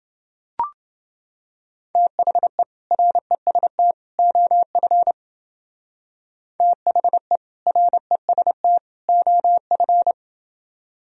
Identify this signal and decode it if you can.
Morse code, heard as a keyed tone, about 28 wpm, 700 Hz. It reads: THE REST OF THE REST OF